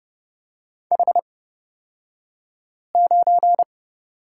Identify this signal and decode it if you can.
Morse code, heard as a keyed tone, about 30 wpm, spaced as for 5 wpm, 700 Hz. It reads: H9
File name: Lesson 36d - BT - Two Characters 30wpm